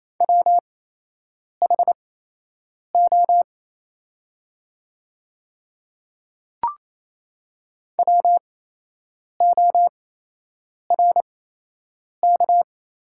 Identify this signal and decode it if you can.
Morse code, heard as a keyed tone, about 28 wpm, spaced as for 8 wpm, 700 Hz. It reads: WHO WORK
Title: 8x Character Spacing - Top 300 Words - 28wpm